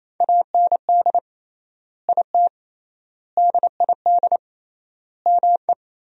Morse code, heard as a keyed tone, about 28 wpm, 700 Hz. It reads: AND IT DID ME
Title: 3x Word-Spacing - Sentences from Top 200 Words - 28wpm